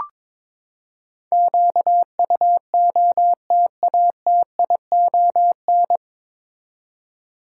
Morse code, heard as a keyed tone, about 22 wpm, 700 Hz. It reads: QUOTATION
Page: Morse Code Ninja practice 22wpm